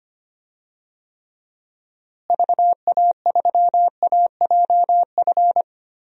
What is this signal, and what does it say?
Morse code, heard as a keyed tone, about 25 wpm, 700 Hz. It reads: VA3AJF
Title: DXCC-100 Call Signs - 6 Characters - Repeated 3x - 25wpm